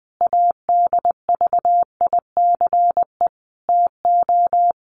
Morse code, heard as a keyed tone, about 20 wpm, 700 Hz. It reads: ADVICE TO
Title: Sentences from Top 1100 Words 20wpm